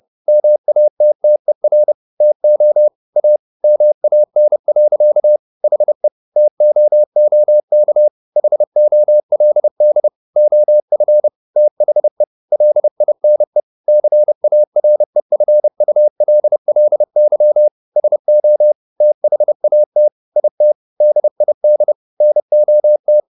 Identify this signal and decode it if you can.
Morse code, heard as a keyed tone, about 30 wpm, 600 Hz. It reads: MATTER TO A MAN. HE TOOK HOLD OF THE LINE CAREFULLY SO THAT IT DID NOT